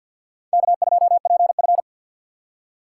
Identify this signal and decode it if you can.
Morse code, heard as a keyed tone, about 48 wpm, 700 Hz. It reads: K2PF
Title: USA Call Signs - 2x Character Spacing - 50wpm